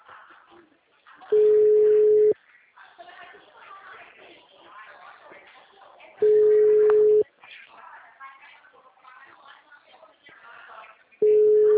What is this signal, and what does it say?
Ringback tone (Europe, 425 Hz), with speech.